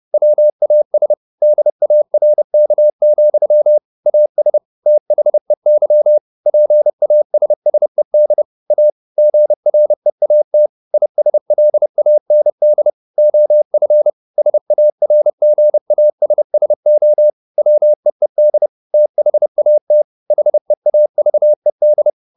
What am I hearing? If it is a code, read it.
Morse code, heard as a keyed tone, about 30 wpm, 600 Hz. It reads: WAS DARK, AS THEY PASSED A GREAT ISLAND OF SARGASSO WEED THAT HEAVED